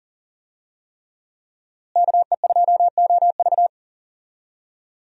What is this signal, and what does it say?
Morse code, heard as a keyed tone, about 40 wpm, 700 Hz. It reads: KE2OV
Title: USA Call Signs - 5 Characters - Repeated 3x - 40wpm